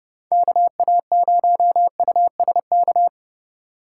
Morse code, heard as a keyed tone, about 29 wpm, 700 Hz. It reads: KA0USK